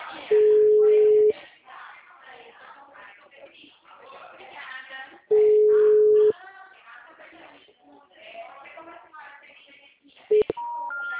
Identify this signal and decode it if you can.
Ringback tone (Europe, 425 Hz), with speech.
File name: out- -2010- - - .67829